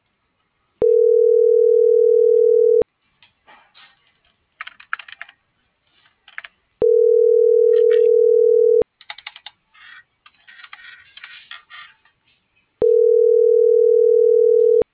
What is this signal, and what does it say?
Ringback tone (North America), 440 + 480 Hz.